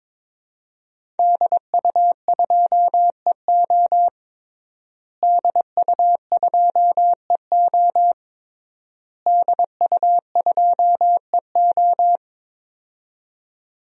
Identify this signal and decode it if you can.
Morse code, heard as a keyed tone, about 22 wpm, 700 Hz. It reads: DU2EO DU2EO DU2EO
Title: DXCC-100 Call Signs - 5 Characters - Repeated 3x - 22wpm